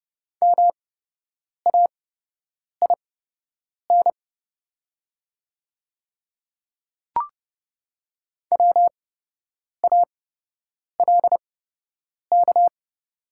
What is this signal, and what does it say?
Morse code, heard as a keyed tone, about 30 wpm, spaced as for 8 wpm, 690 Hz. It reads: MAIN WALK
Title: 8x Character Spacing - Top 500 Words - 30wpm